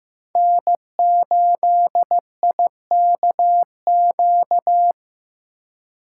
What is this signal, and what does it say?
Morse code, heard as a keyed tone, about 17 wpm, 700 Hz. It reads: N8IKQ
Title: Speed Racing - USA 1x3 Call Signs 15wpm